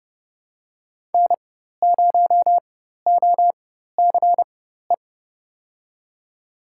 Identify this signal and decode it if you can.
Morse code, heard as a keyed tone, about 30 wpm, spaced as for 14 wpm, 700 Hz. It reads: N0OCE
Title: USA Call Signs - 5 Characters - 4x Character Spacing - 30wpm